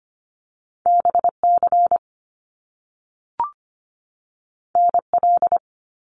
Morse code, heard as a keyed tone, about 25 wpm, 700 Hz. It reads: BC NL
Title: Canadian Provinces and Territories 25wpm